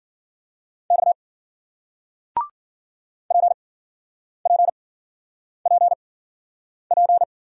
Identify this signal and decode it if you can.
Morse code, heard as a keyed tone, about 56 wpm, spaced as for 9 wpm, 700 Hz. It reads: X PPPP